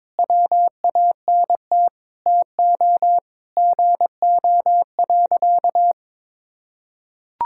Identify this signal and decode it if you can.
Morse code, heard as a keyed tone, about 22 wpm, 700 Hz. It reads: WANT TO GO.
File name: practice / 22wpm / Sentences with 3-Word Phrases - 22wpm